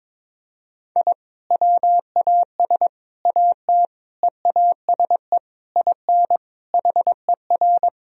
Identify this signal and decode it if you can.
Morse code, heard as a keyed tone, about 21 wpm, 700 Hz. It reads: I WAS AT EASE IN HER
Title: Sentences from Top 1000 Words - Review Part 1 - 22wpm